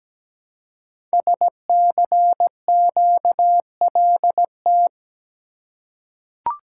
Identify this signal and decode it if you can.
Morse code, heard as a keyed tone, about 17 wpm, 700 Hz. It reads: SCQLT
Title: Code Groups - 5 Letters - 17wpm